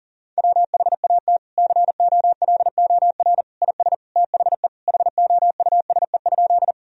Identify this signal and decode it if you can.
Morse code, heard as a keyed tone, about 40 wpm, 700 Hz. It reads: WHAT COLOR IS THE HOUSE?
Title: Sentences from Top 1000 Words - Review Part 2 - 40wpm